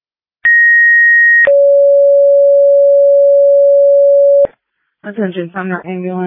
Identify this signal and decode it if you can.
Two-tone page, heard as a one-second tone, then a three-second tone, about 1900 then 560 Hz, with speech. It is heard from 0.4 to 4.5 s.